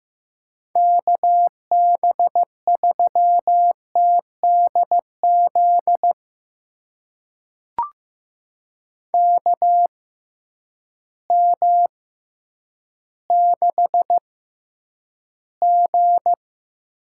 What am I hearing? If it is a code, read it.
Morse code, heard as a keyed tone, about 15 wpm, spaced as for 9 wpm, 700 Hz. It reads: KB3TDZ KM6G
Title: USA Call Signs - 6 Characters - 6x Character Spacing - 15wpm